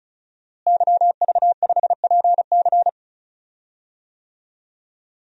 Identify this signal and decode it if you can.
Morse code, heard as a keyed tone, about 36 wpm, 700 Hz. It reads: YV5PC
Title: DXCC-100 Call Signs - 5 Characters - Repeated 3x - 35wpm